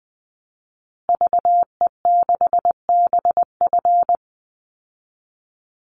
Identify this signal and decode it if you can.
Morse code, heard as a keyed tone, about 20 wpm, 700 Hz. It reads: VE6BF
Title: Call Signs Repeated 2x - 20wpm